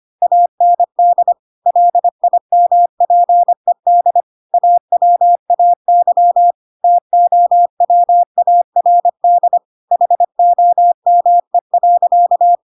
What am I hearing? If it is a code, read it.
Morse code, heard as a keyed tone, about 25 wpm, 700 Hz. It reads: AND LIMPED AWAY TOWARD HOME.
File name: Pinocchio / Pinocchio ch2 25 25 ik1hge